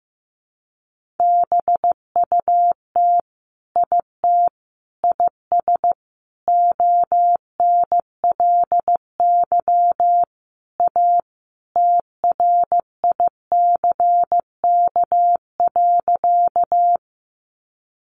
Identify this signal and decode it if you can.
Morse code, heard as a keyed tone, about 15 wpm, 700 Hz. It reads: BUT IT IS ONLY A TRICK.
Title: Sentences from Top 1500 Words 15wpm